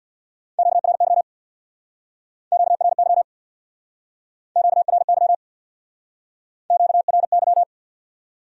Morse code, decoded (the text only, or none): QRY QRY QRY QRY